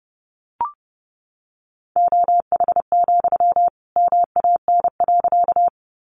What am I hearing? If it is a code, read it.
Morse code, heard as a keyed tone, about 28 wpm, 700 Hz. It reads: OH, MAN.
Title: Sentences from Top 500 Words 30wpm